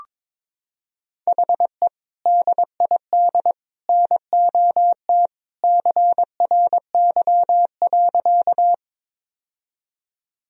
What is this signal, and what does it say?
Morse code, heard as a keyed tone, about 22 wpm, 700 Hz. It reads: HE DID NOT CRY.
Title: Sentences from Top 1000 Words - Review Part 2 - 22wpm